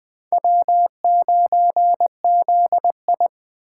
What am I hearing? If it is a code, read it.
Morse code, heard as a keyed tone, about 24 wpm, 700 Hz. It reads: W9ZI
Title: Speed Racing - USA 1x2 Call Signs 20wpm